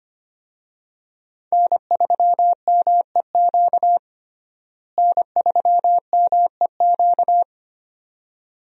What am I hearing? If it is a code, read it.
Morse code, heard as a keyed tone, about 25 wpm, 700 Hz. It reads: N3MEQ N3MEQ
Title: USA Call Signs - 5 Characters - Repeated 2x - 25wpm